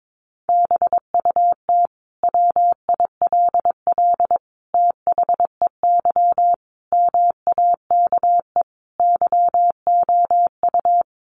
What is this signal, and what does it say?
Morse code, heard as a keyed tone, about 22 wpm, 700 Hz. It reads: BUT WILL THEY MAKE YOU
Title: Sentences from Top 1000 Words 22wpm